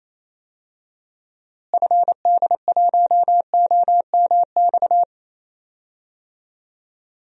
Morse code, heard as a keyed tone, about 27 wpm, 700 Hz. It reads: FD1OMX
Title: DXCC-100 Call Signs - 6 Characters - Repeated 2x - 28wpm